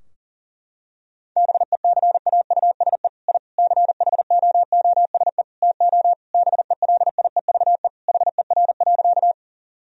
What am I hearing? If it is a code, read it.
Morse code, heard as a keyed tone, about 40 wpm, 700 Hz. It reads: BECAUSE I CHOOSE TO BELIEVE HER.
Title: Sentences from Top 1000 Words - Review Part 4 - 40wpm